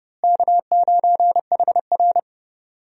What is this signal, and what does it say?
Morse code, heard as a keyed tone, about 32 wpm, 700 Hz. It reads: K9HR
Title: USA Call Signs - 4 Characters - Repeated 3x - 30wpm